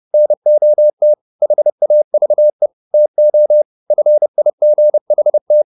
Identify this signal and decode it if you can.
Morse code, heard as a keyed tone, about 30 wpm, 600 Hz. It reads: NOT HAVE TO FIGHT